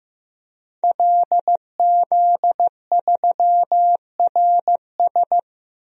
Morse code, heard as a keyed tone, about 15 wpm, 700 Hz. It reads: LZ3RS
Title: DXCC-100 Call Signs - 5 Characters - 15wpm